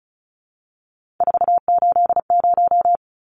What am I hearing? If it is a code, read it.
Morse code, heard as a keyed tone, about 34 wpm, 700 Hz. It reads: 480
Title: Numbers - 3-Digit 35wpm